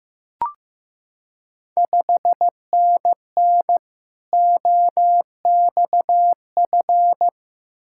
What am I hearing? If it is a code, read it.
Morse code, heard as a keyed tone, about 15 wpm, 690 Hz. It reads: 5NN OXF
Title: Ontario QSO Party Exchange 15wpm